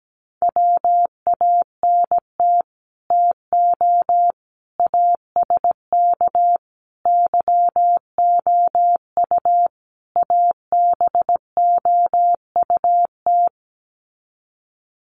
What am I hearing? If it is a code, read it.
Morse code, heard as a keyed tone, about 17 wpm, 700 Hz. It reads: WANT TO ASK YOU ABOUT
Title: Sets of 5 Words Encore2 17wpm